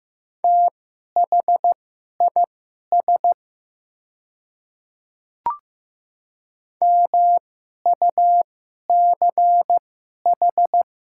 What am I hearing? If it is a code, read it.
Morse code, heard as a keyed tone, about 15 wpm, spaced as for 11 wpm, 700 Hz. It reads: THIS MUCH
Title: 2x Character Spacing - Top 300 Words - 15wpm